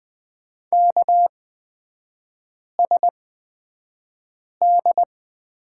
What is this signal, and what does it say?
Morse code, heard as a keyed tone, about 20 wpm, spaced as for 5 wpm, 700 Hz. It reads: KSD
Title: 224 Lesson 36e - BT - Three Characters 20wpm